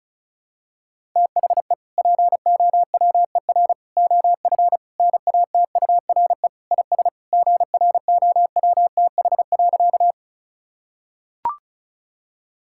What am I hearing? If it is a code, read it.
Morse code, heard as a keyed tone, about 36 wpm, 700 Hz. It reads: THE POWER OF NATURE IS GROWTH.
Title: Sentences from Top 1000 Words - Review Part 5 - 35wpm